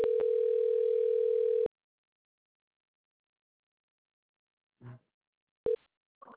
Ringback tone (North America), 440 + 480 Hz.